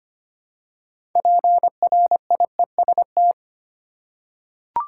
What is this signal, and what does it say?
Morse code, heard as a keyed tone, about 25 wpm, 700 Hz. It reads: PRIEST
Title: Six Letter Words - 25wpm